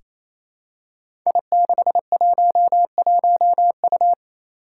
Morse code, heard as a keyed tone, about 27 wpm, 700 Hz. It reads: I611U
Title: Five Challenging Letter-Number Combinations 28wpm